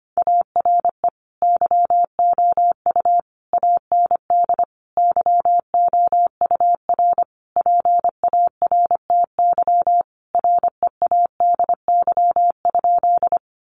Morse code, heard as a keyed tone, about 25 wpm, 700 Hz. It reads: ARE YOU AND YOUR PARTY READY?
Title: Sentences from Top 1400 Words 25wpm